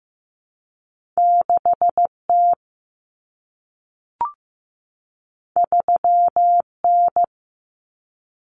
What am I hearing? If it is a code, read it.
Morse code, heard as a keyed tone, about 15 wpm, 700 Hz. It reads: 6 T 3 N
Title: Common Cut Numbers - 2-Digit 15wpm